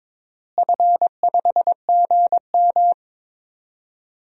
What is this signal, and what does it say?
Morse code, heard as a keyed tone, about 22 wpm, 700 Hz. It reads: F5GM
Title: DXCC-100 Call Signs - 4 Characters - Repeated 2x - 22wpm